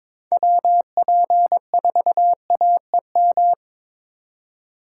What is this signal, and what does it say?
Morse code, heard as a keyed tone, about 22 wpm, 700 Hz. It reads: WP4AEM
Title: USA Call Signs - 6 Characters - 6x Character Spacing - 22wpm